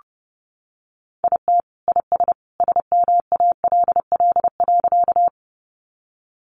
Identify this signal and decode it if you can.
Morse code, heard as a keyed tone, about 30 wpm, 700 Hz. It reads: IT IS SMALL.
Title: Easy Phrases 30wpm